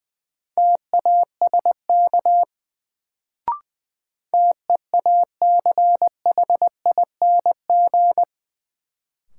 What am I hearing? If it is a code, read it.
Morse code, heard as a keyed tone, about 21 wpm, 700 Hz. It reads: TASK TEACHING
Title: Sentences from Top 1100 Words - New Word Drill Alt - 20wpm